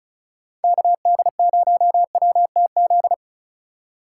Morse code, heard as a keyed tone, about 34 wpm, 700 Hz. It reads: KD0WTZ
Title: USA Call Signs - 6 Characters - 2x Character Spacing - 35wpm